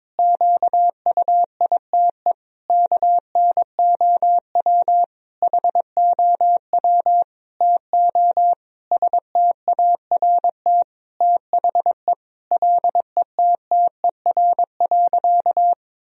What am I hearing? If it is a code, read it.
Morse code, heard as a keyed tone, about 22 wpm, 700 Hz. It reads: QUITE KNOW HOW TO START THE LETTER.
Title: Sentences from Top 1000 Words - Review Part 1 - 22wpm